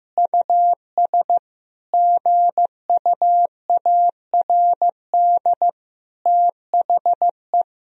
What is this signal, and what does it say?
Morse code, heard as a keyed tone, about 15 wpm, 700 Hz. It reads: US GUARD THE
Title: Sentences from Top 1000 Words - Review Part 4 - 15wpm